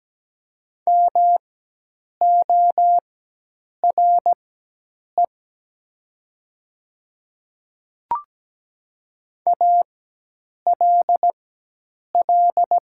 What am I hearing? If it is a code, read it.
Morse code, heard as a keyed tone, about 17 wpm, spaced as for 8 wpm, 700 Hz. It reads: MORE ALL
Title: 4x Character Spacing - Top 100 Words - 17wpm